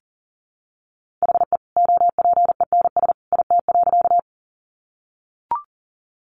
Morse code, heard as a keyed tone, about 40 wpm, 700 Hz. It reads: HE OPENS IT.
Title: Additional Easy Phrases 40wpm